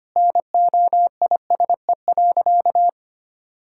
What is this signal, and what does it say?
Morse code, heard as a keyed tone, about 25 wpm, 700 Hz. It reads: NOISE.
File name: practice / 25wpm / Sentences with 4-Word Phrases - 25wpm